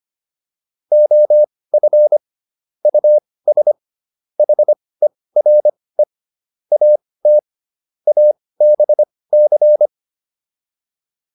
Morse code, heard as a keyed tone, about 25 wpm, spaced as for 18 wpm, 600 Hz. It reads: OF US HERE AT ABC